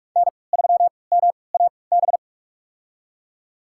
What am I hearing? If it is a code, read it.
Morse code, heard as a keyed tone, about 42 wpm, spaced as for 25 wpm, 700 Hz. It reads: N3MAB
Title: USA Call Signs - 5 Characters - 3x Character Spacing - 45wpm